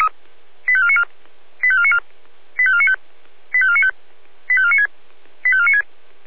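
Five-tone selective call (ZVEI), heard as quick runs of five tones, tones of about 70 ms. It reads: 85494 85494 85495 85495 85496 85496
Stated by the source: ZVEI five-tone selective call received on SDR.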